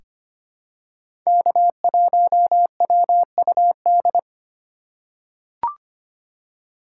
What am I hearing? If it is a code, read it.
Morse code, heard as a keyed tone, about 25 wpm, 700 Hz. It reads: K1WUD
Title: USA Call Signs - 5 Characters - 2x Character Spacing - 25wpm